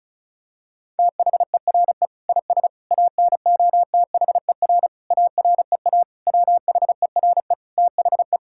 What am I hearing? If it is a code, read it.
Morse code, heard as a keyed tone, about 36 wpm, 700 Hz. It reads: THERE IS ANOTHER AREA WHERE THE
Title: Sentences from Top 1000 Words - Review Part 5 - 35wpm